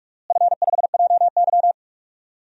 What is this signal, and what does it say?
Morse code, heard as a keyed tone, about 47 wpm, 700 Hz. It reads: F5JY
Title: DXCC-100 Call Signs - 4 Characters - Repeated 2x - 45wpm